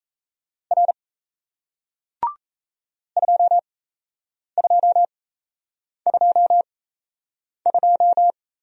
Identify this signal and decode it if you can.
Morse code, heard as a keyed tone, about 38 wpm, spaced as for 8 wpm, 700 Hz. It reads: R 2222